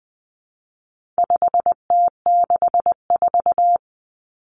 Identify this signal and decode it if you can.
Morse code, heard as a keyed tone, about 20 wpm, 700 Hz. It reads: 5T64
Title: Common Cut Numbers - 4-Digit 20wpm